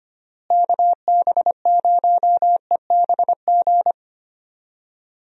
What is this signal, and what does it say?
Morse code, heard as a keyed tone, about 25 wpm, 700 Hz. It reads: KB0EBG
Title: USA Call Signs - 6 Characters - Repeated 2x - 25wpm